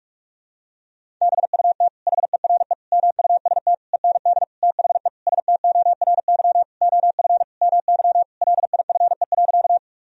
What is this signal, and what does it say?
Morse code, heard as a keyed tone, about 46 wpm, 700 Hz. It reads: BUT HERE MUST END THE STORY OF MY LIFE.
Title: Sentences from Top 1000 Words - Review Part 1 - 45wpm